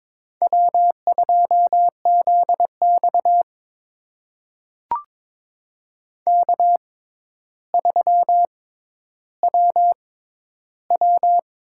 Morse code, heard as a keyed tone, about 22 wpm, spaced as for 11 wpm, 700 Hz. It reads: W2ZX K3WW
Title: USA Call Signs - 4 Characters - 6x Character Spacing - 22wpm